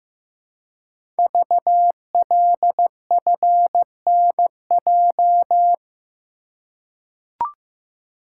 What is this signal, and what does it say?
Morse code, heard as a keyed tone, about 15 wpm, 700 Hz. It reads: VLFNJ
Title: Code Groups - 5 Letters - 15wpm